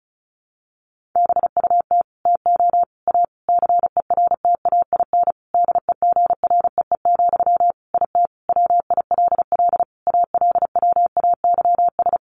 Morse code, heard as a keyed tone, about 34 wpm, 700 Hz. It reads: BUT TO A CERTAIN DEGREE, IT WILL ALWAYS
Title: Sentences from Top 1500 Words 35wpm